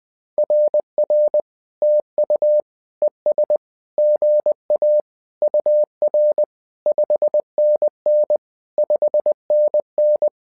Morse code, heard as a keyed tone, about 20 wpm, 600 Hz. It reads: RR TU ES GA UR 5NN 5NN
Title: SOTA QSO Practice 20wpm